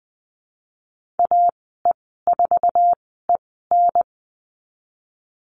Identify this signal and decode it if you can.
Morse code, heard as a keyed tone, about 20 wpm, spaced as for 15 wpm, 700 Hz. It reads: AE4EN
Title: 055 Lesson 10g - Letter r - Call Signs 20wpm